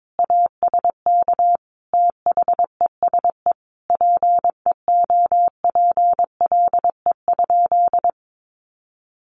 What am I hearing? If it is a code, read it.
Morse code, heard as a keyed tone, about 22 wpm, 700 Hz. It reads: ASK THESE PEOPLE?
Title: Sentences from Top 200 Words 22wpm